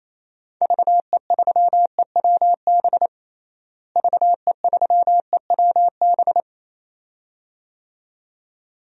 Morse code, heard as a keyed tone, about 28 wpm, 700 Hz. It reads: VE3EWB VE3EWB
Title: DXCC-100 Call Signs - 6 Characters - Repeated 2x - 28wpm